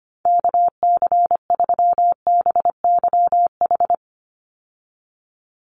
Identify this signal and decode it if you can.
Morse code, heard as a keyed tone, about 25 wpm, 700 Hz. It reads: KC3BYH
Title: USA Call Signs 25wpm